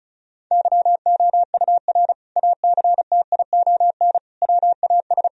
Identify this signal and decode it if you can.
Morse code, heard as a keyed tone, about 36 wpm, 700 Hz. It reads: YOUR ACTION WAS
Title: Sentences from Top 1000 Words - Review Part 4 - 35wpm